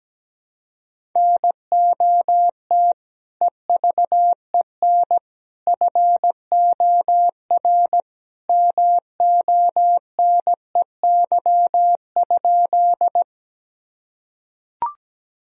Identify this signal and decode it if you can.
Morse code, heard as a keyed tone, about 17 wpm, 700 Hz. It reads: NOT EVEN FOR MONEY?
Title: Sentences from Top 1000 Words - Review Part 1 - 17wpm